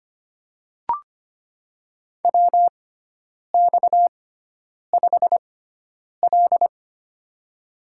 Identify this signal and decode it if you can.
Morse code, heard as a keyed tone, about 25 wpm, spaced as for 9 wpm, 700 Hz. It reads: WX5L
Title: USA Call Signs - 4 Characters - 6x Character Spacing - 25wpm